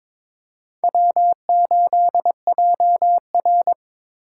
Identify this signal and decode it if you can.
Morse code, heard as a keyed tone, about 26 wpm, 700 Hz. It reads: W8JR
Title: Speed Racing - USA Call Signs 22wpm